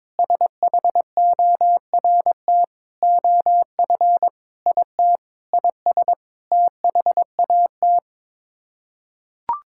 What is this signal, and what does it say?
Morse code, heard as a keyed tone, about 21 wpm, 700 Hz. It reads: SHORT OF IT IS THAT
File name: practice / 22wpm / Sets of 8 Words - 22wpm